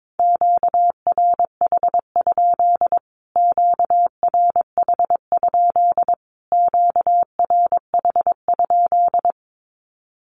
Morse code, heard as a keyed tone, about 22 wpm, 700 Hz. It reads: QRH? QRH? QRH?
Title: Q-Codes Repeated 3x 22wpm